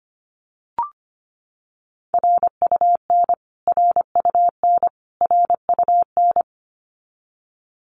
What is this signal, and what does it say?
Morse code, heard as a keyed tone, about 25 wpm, 700 Hz. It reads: RUN RUN RUN
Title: Top 300 Words Repeated 25wpm